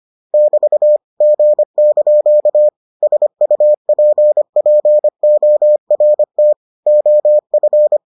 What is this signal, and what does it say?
Morse code, heard as a keyed tone, about 25 wpm, 600 Hz. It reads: <BT> G) SUPPORT OF